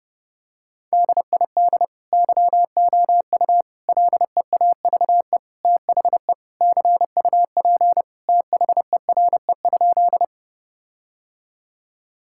Morse code, heard as a keyed tone, about 32 wpm, 700 Hz. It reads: DID YOU LEAVE THE CUP THERE?
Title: Sentences from Top 1600 Words 30wpm